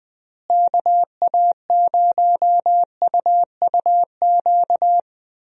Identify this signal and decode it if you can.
Morse code, heard as a keyed tone, about 20 wpm, 700 Hz. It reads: KA0UUQ
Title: USA Call Signs - 6 Characters - Repeated 3x - 20wpm